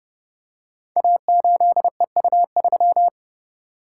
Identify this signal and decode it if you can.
Morse code, heard as a keyed tone, about 30 wpm, 700 Hz. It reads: A8EU3
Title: Code Groups - 5 Letter:Numbers - 30wpm